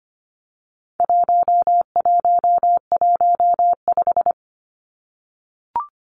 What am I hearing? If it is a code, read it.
Morse code, heard as a keyed tone, about 25 wpm, 700 Hz. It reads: 1115
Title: Numbers - 4-Digit 25wpm